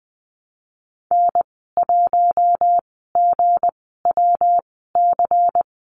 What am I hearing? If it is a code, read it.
Morse code, heard as a keyed tone, about 20 wpm, spaced as for 14 wpm, 700 Hz. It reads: N1GWC
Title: 238 Lesson 38g - Letter q - Call Signs 20wpm